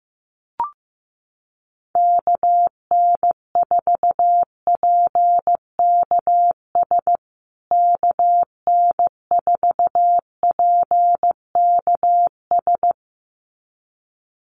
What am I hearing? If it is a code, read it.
Morse code, heard as a keyed tone, about 15 wpm, 700 Hz. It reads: KN4PKS KN4PKS
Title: USA Call Signs Repeated 2x 15wpm